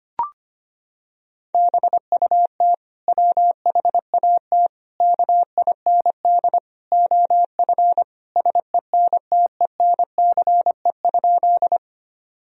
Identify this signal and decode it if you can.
Morse code, heard as a keyed tone, about 25 wpm, 700 Hz. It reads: BUT WHAT KIND OF SENTENCE?
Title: Sentences from Top 1000 Words - Review Part 1 - 25wpm